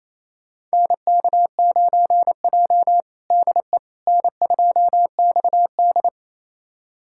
Morse code, heard as a keyed tone, about 27 wpm, 700 Hz. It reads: NK9J DE N2XD
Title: USA Callsign DE Callsign - 4 Characters - 28wpm